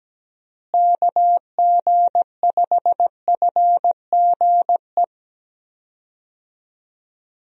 Morse code, heard as a keyed tone, about 17 wpm, 700 Hz. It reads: KG5FGE
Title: USA Call Signs - 6 Characters - Repeated 2x - 17wpm